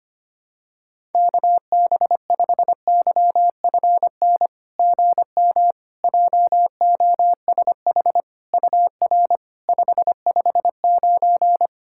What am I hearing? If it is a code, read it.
Morse code, heard as a keyed tone, about 25 wpm, 690 Hz. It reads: KB5YFN GM JOSH UR 559
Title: SOTA QSO Practice 25wpm